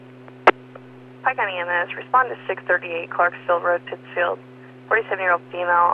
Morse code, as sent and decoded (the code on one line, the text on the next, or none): none